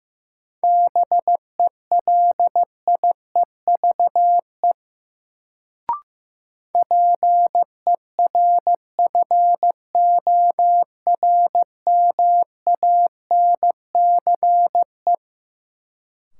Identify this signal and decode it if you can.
Morse code, heard as a keyed tone, about 15 wpm, 700 Hz. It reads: BELIEVE PERFORMANCE
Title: Sentences from Top 1000 Words - New Word Drill - 15wpm